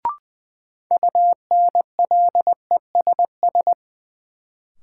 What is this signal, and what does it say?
Morse code, heard as a keyed tone, about 20 wpm, 700 Hz. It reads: UNLESS
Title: Sentences from Top 1000 Words - Review Part 5 - New Word Drill - 20wpm